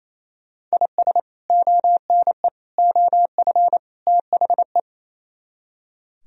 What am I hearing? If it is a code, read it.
Morse code, heard as a keyed tone, about 35 wpm, 700 Hz. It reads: IS ONE OF THE